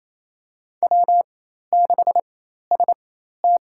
Morse code, heard as a keyed tone, about 28 wpm, spaced as for 13 wpm, 700 Hz. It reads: W6ST
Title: USA Call Signs - 4x Character Spacing - 28wpm